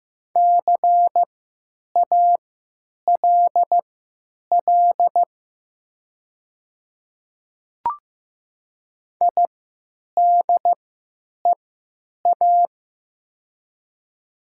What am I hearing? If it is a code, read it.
Morse code, heard as a keyed tone, about 15 wpm, spaced as for 9 wpm, 700 Hz. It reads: CALL IDEA
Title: 3x Character Spacing - Top 300 Words - 15wpm